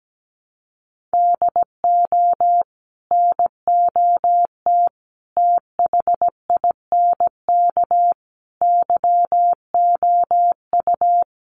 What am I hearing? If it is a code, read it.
Morse code, heard as a keyed tone, about 17 wpm, 700 Hz. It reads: DO NOT THINK YOU
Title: Sets of 4 Words 17wpm